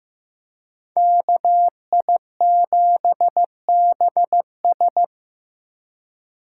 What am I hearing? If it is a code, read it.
Morse code, heard as a keyed tone, about 15 wpm, 700 Hz. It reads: KI7BS
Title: USA Call Signs - 5 Characters - 2x Character Spacing - 15wpm